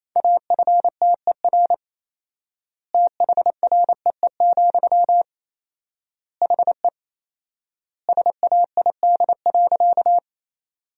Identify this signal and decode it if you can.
Morse code, heard as a keyed tone, about 28 wpm, 700 Hz. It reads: AFTER THREE, HE SAID.
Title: 4x Word-Spacing - Sentences from Top 200 Words - 28wpm